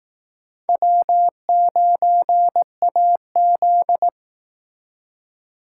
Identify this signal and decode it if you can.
Morse code, heard as a keyed tone, about 18 wpm, 700 Hz. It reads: W9AZ
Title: Speed Racing - USA 1x2 Call Signs 15wpm